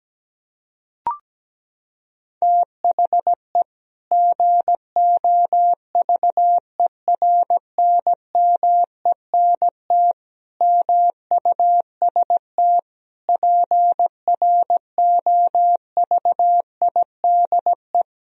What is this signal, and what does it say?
Morse code, heard as a keyed tone, about 17 wpm, 700 Hz. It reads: THE GOVERNMENT MUST PROVIDE
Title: Sentences from Top 1000 Words - Review Part 3 - 17wpm